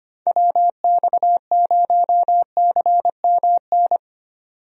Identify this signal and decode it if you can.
Morse code, heard as a keyed tone, about 25 wpm, 700 Hz. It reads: WX0CMN
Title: USA Call Signs - 6 Characters - 25wpm